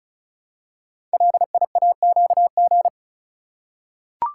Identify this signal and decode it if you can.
Morse code, heard as a keyed tone, about 36 wpm, 700 Hz. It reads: LIAQG
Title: Code Groups - 5 Letters - 35wpm